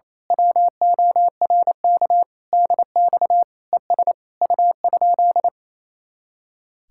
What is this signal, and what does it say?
Morse code, heard as a keyed tone, about 27 wpm, 700 Hz. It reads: WORK DX ES U?
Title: Speed-Racing Common QSO Phrases Encore - 28wpm